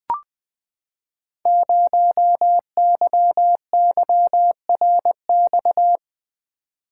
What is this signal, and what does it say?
Morse code, heard as a keyed tone, about 20 wpm, 700 Hz. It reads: 0YYRX
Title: Code Groups - 5 Letter:Numbers - 20wpm